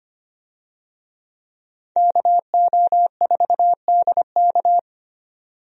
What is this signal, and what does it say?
Morse code, heard as a keyed tone, about 25 wpm, 700 Hz. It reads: KO4DK
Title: USA Call Signs - 5 Characters - 25wpm